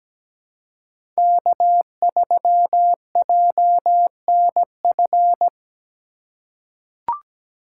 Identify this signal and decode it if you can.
Morse code, heard as a keyed tone, about 17 wpm, 700 Hz. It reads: K3JNF
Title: USA Call Signs - 5 Characters - 2x Character Spacing - 17wpm